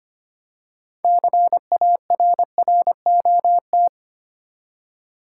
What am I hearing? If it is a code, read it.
Morse code, heard as a keyed tone, about 25 wpm, 700 Hz. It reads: CARROT
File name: practice / 25wpm / Easy Words with Hints - 25wpm